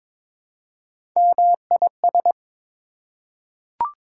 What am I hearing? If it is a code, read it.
Morse code, heard as a keyed tone, about 22 wpm, 700 Hz. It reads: MIS
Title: Prefixes and Suffixes 22wpm